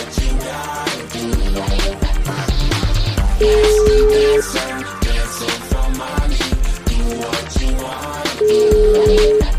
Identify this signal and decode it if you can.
Ringback tone (Europe, 425 Hz), with noise.